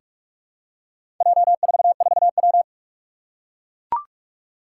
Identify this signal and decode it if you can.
Morse code, heard as a keyed tone, about 44 wpm, 700 Hz. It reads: J44W
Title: Four Challenging Letter-Number Combinations 45wpm